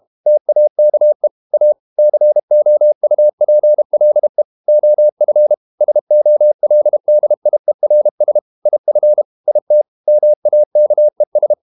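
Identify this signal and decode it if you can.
Morse code, heard as a keyed tone, about 32 wpm, 600 Hz. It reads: TAKE A COUPLE OF SOLDIERS IF IT MAKES
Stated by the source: Morse code practice at 32 wpm from CW med Gnister, reading Ken Follett's novel Never in English.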